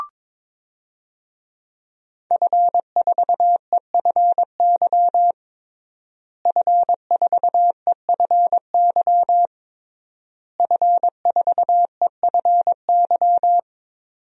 Morse code, heard as a keyed tone, about 22 wpm, 700 Hz. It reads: F4EFY F4EFY F4EFY
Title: DXCC-100 Call Signs - 5 Characters - Repeated 3x - 22wpm